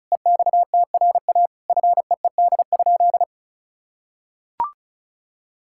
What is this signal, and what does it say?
Morse code, heard as a keyed tone, about 34 wpm, 700 Hz. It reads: EXTRA FEED?
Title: Sentences from Top 1600 Words 35wpm